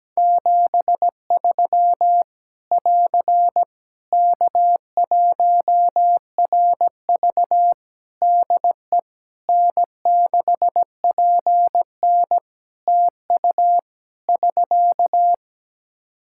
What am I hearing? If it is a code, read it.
Morse code, heard as a keyed tone, about 17 wpm, 700 Hz. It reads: 73 <AR> K1RV DE N6PN TU <SK>
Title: Practice QSO - Exchange 4 - Ending - 17wpm